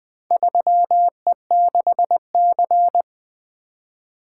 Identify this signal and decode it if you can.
Morse code, heard as a keyed tone, about 20 wpm, 700 Hz. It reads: 3E6C
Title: Four Letter-Number Combinations - 20wpm